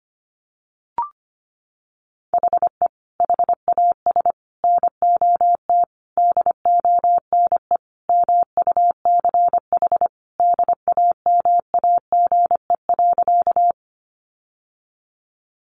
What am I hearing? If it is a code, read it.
Morse code, heard as a keyed tone, about 25 wpm, 700 Hz. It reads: HE HAS NOT DONE MUCH DAMAGE.
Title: Sentences from Top 1100 Words 25wpm